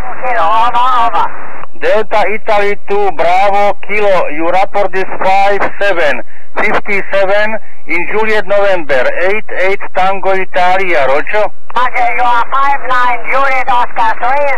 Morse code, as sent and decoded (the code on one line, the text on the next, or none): none